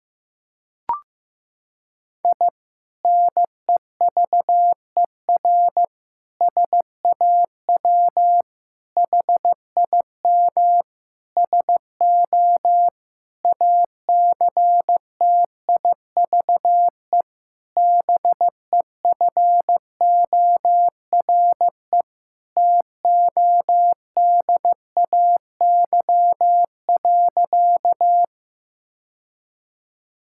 Morse code, heard as a keyed tone, about 15 wpm, 700 Hz. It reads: I NEVER SAW HIM SO ACTIVE BEFORE TODAY.
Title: Sentences from Top 1000 Words - Review Part 6 - 15wpm